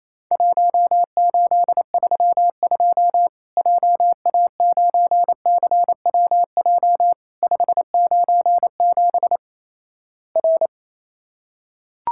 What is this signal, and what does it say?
Morse code, heard as a keyed tone, about 28 wpm, 700 Hz. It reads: 1832 JA9CWJ 597 R
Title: Worked All Europe DX Contest - QTC Exchange - 28wpm